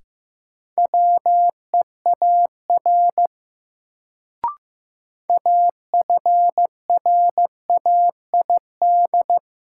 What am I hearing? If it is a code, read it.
Morse code, heard as a keyed tone, about 15 wpm, 700 Hz. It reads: WEAR AFRAID
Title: Sentences from Top 1500 Words - New Word Drill Alt - 15wpm